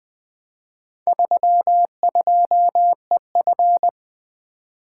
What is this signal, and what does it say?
Morse code, heard as a keyed tone, about 20 wpm, 700 Hz. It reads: 32EF
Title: Four Letter-Number Combinations - 20wpm